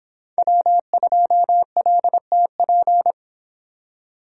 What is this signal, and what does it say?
Morse code, heard as a keyed tone, about 26 wpm, 700 Hz. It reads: W2LTP